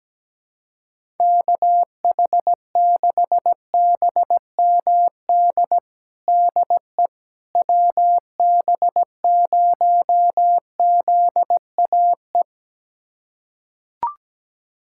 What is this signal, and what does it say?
Morse code, heard as a keyed tone, about 17 wpm, 700 Hz. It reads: KH6BMD DE WB0ZAE
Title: USA Callsign DE Callsign - 6 Characters - 17wpm